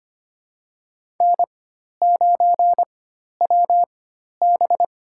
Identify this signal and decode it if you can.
Morse code, heard as a keyed tone, about 25 wpm, spaced as for 12 wpm, 700 Hz. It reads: N9WB
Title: USA Call Signs - 4 Characters - 4x Character Spacing - 25wpm